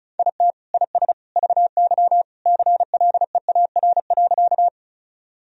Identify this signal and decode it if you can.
Morse code, heard as a keyed tone, about 36 wpm, 700 Hz. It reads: IT IS VY CLEAR.
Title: Sentences from Top 500 Words and Abbreviations 35wpm